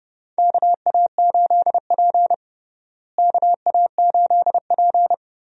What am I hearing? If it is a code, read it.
Morse code, heard as a keyed tone, about 32 wpm, 700 Hz. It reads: KA8P KA8P
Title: USA Call Signs - 4 Characters - Repeated 2x - 30wpm